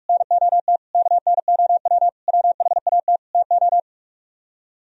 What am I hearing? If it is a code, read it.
Morse code, heard as a keyed tone, about 50 wpm, 700 Hz. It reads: NOT KNOW WHAT TO